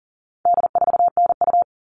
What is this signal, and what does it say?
Morse code, heard as a keyed tone, about 40 wpm, 700 Hz. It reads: D4NU
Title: All Cut Numbers - 4-Digit 40wpm